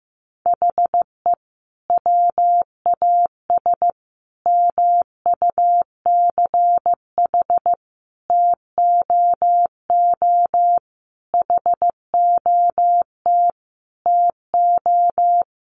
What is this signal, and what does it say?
Morse code, heard as a keyed tone, about 15 wpm, 700 Hz. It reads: HE WAS MUCH TOO HOT TO
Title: Sentences from Top 1200 Words 15wpm